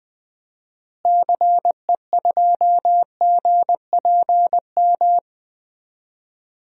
Morse code, heard as a keyed tone, about 20 wpm, 700 Hz. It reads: CE2GPM
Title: DXCC-100 Call Signs - 6 Characters - Repeated 3x - 20wpm